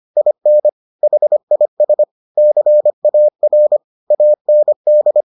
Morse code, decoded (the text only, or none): IN HIS CAR AND